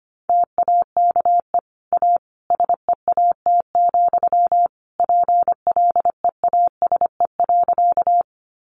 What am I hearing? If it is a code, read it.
Morse code, heard as a keyed tone, about 25 wpm, 700 Hz. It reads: TAKE A SEAT, PLEASE.
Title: Sentences from Top 1500 Words 25wpm